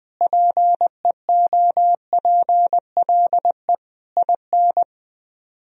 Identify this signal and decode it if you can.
Morse code, heard as a keyed tone, about 20 wpm, 700 Hz. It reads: PEOPLE IN
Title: Speed-Racing - Sets of 5 Words 20wpm